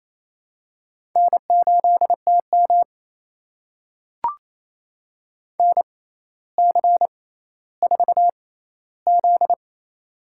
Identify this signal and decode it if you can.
Morse code, heard as a keyed tone, about 28 wpm, spaced as for 14 wpm, 700 Hz. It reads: N8TM NC4Z
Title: USA Call Signs - 4 Characters - 6x Character Spacing - 28wpm